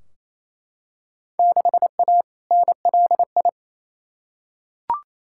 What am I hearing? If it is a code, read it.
Morse code, heard as a keyed tone, about 28 wpm, 700 Hz. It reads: 6A NLI
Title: Field Day Exchanges Encore - 28wpm